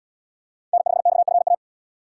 Morse code, heard as a keyed tone, about 77 wpm, 700 Hz. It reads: N5XFA